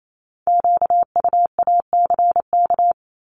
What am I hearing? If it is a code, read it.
Morse code, heard as a keyed tone, about 28 wpm, 700 Hz. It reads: QUACK
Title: Multi-letter Phonemes 28wpm